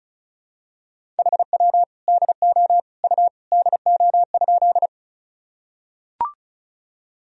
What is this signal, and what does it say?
Morse code, heard as a keyed tone, about 36 wpm, 700 Hz. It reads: HW DO U DO?
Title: Sentences from Top 100 Words and Abbreviations 35wpm